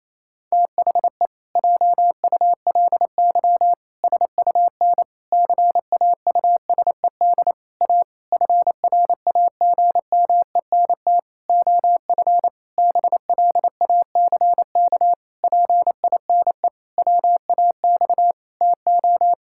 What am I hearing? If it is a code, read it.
Morse code, heard as a keyed tone, about 28 wpm, 700 Hz. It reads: THE JULY SUN CAUSED A FRAGMENT OF BLACK PINE WAX TO
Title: Pangram Challenge v2 - 28wpm